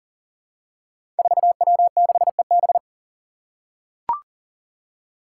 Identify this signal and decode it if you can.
Morse code, heard as a keyed tone, about 40 wpm, 700 Hz. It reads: 4W6EB
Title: Five Challenging Letter-Number Combinations 40wpm